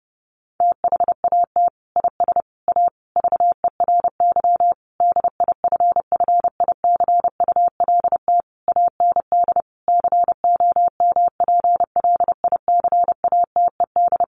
Morse code, heard as a keyed tone, about 30 wpm, 700 Hz. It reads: THAT IS A VERY DIFFICULT AND COMPLICATED
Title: Sentences from Top 1200 Words 30wpm